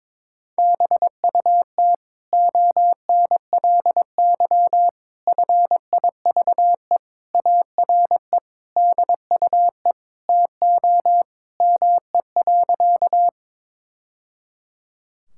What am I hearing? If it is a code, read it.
Morse code, heard as a keyed tone, about 21 wpm, 700 Hz. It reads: BUT ONLY FIVE ARE DUE TO ME.
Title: Sentences from Top 1000 Words - Review Part 6 - 22wpm